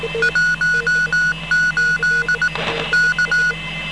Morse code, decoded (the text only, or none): 18 K